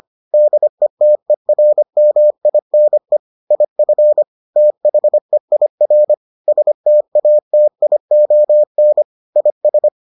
Morse code, heard as a keyed tone, about 25 wpm, 600 Hz. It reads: DETERMINE IF THEIR STATION IS